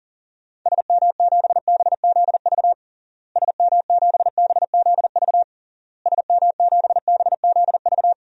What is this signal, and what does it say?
Morse code, heard as a keyed tone, about 40 wpm, 700 Hz. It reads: SM7BZV SM7BZV SM7BZV
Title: DXCC-100 Call Signs - 6 Characters - Repeated 3x - 40wpm